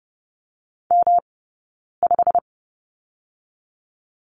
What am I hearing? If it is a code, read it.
Morse code, heard as a keyed tone, about 30 wpm, 700 Hz. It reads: M 5
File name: Lesson 18d - Letter m - Two Characters 30wpm